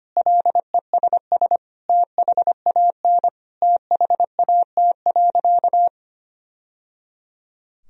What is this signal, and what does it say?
Morse code, heard as a keyed tone, about 25 wpm, 700 Hz. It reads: LESS THAN THAT.
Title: Sentences from Top 1000 Words - Review Part 2 - 25wpm